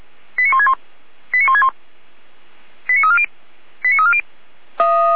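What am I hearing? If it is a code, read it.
Five-tone selective call (ZVEI), heard as quick runs of five tones, tones of about 70 ms. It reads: 89161 89161 89340 89340
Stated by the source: Received on an SDR; ZVEI five-tone call.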